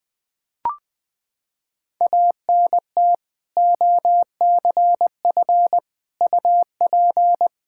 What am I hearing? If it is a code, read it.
Morse code, heard as a keyed tone, about 20 wpm, 700 Hz. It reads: ANT OCF UP
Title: Practice QSO - Exchange 2 - Info - 20wpm